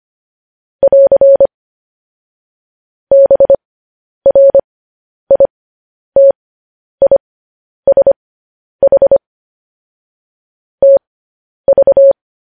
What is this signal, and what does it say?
Morse code, heard as a keyed tone, about 25 wpm, spaced as for 10 wpm, 550 Hz. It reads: <AR> BRITISH TV
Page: news-headlines-in-morse-code-at-10-wpm